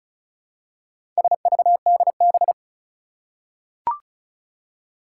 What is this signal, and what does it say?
Morse code, heard as a keyed tone, about 36 wpm, 700 Hz. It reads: SVDB
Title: Four Challenging Letter-Number Combinations 35wpm